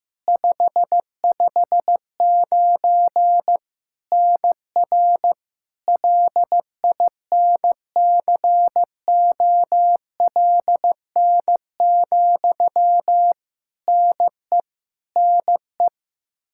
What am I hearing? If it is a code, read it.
Morse code, heard as a keyed tone, about 15 wpm, 700 Hz. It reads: 559 NR LINCOLN, NE NE
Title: Practice QSO - Exchange 1 - Beginning - 15wpm